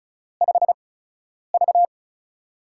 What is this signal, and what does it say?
Morse code, heard as a keyed tone, about 36 wpm, spaced as for 10 wpm, 700 Hz. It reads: 5V
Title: USA Call Signs - 4 Characters - 8x Character Spacing - 35wpm